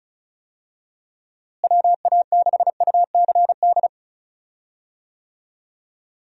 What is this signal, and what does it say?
Morse code, heard as a keyed tone, about 36 wpm, 700 Hz. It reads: WA6UCD
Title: USA Call Signs - 6 Characters - Repeated 3x - 35wpm